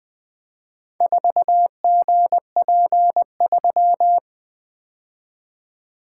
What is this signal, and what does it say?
Morse code, heard as a keyed tone, about 20 wpm, 700 Hz. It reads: 4GP3
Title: Four Letter-Number Combinations - 20wpm